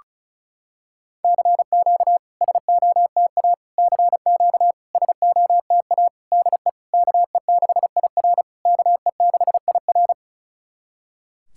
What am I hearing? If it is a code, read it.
Morse code, heard as a keyed tone, about 36 wpm, 690 Hz. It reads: CQ SOTA CQ SOTA DE KE6IR KE6IR
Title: SOTA QSO Practice 35wpm